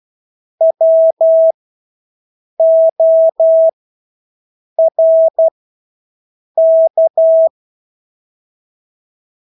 Morse code, heard as a keyed tone, about 12 wpm, spaced as for 6 wpm, 650 Hz. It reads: WORK
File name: Engelske ord 12-6 wpm